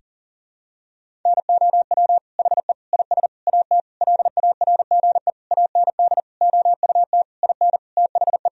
Morse code, heard as a keyed tone, about 40 wpm, 700 Hz. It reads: NOW HE IS AT LARGE AND OUT IN THE
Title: Sentences from Top 1000 Words - Review Part 1 - 40wpm